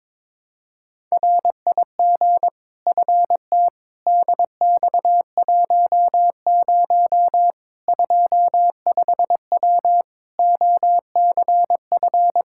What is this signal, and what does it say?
Morse code, heard as a keyed tone, about 22 wpm, 700 Hz. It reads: RIG FT DX10 25W OCF
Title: Practice QSO - Exchange 2 - Info - 22wpm